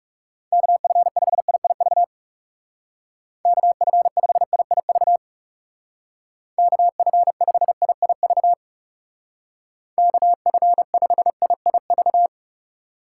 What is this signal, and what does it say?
Morse code, heard as a keyed tone, about 39 wpm, 700 Hz. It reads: KF5IIV KF5IIV KF5IIV KF5IIV